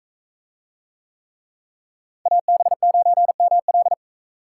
Morse code, heard as a keyed tone, about 41 wpm, 700 Hz. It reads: AB9ML